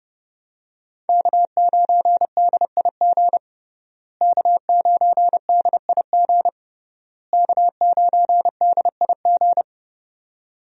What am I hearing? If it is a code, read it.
Morse code, heard as a keyed tone, about 30 wpm, 700 Hz. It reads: K9DIG K9DIG K9DIG
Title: USA Call Signs - 5 Characters - Repeated 3x - 30wpm